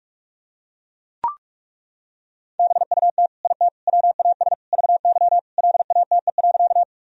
Morse code, heard as a keyed tone, about 41 wpm, 700 Hz. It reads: BUT IT WAS VY LATE.
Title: Sentences from Top 300 Words and Abbreviations 45wpm